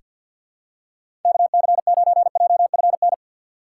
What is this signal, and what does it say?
Morse code, heard as a keyed tone, about 48 wpm, 700 Hz. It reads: KC9JFN
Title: USA Call Signs - 6 Characters - 3x Character Spacing - 50wpm